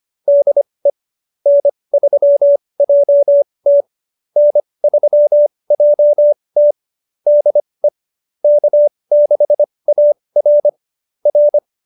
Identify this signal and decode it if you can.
Morse code, heard as a keyed tone, about 25 wpm, spaced as for 20 wpm, 590 Hz. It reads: DE N3JT N3JT DE K6AR R
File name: QSO307 20